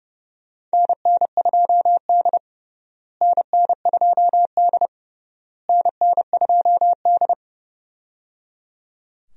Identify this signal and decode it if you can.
Morse code, heard as a keyed tone, about 30 wpm, 700 Hz. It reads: NN2D NN2D NN2D
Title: USA Call Signs - 4 Characters - Repeated 3x - 30wpm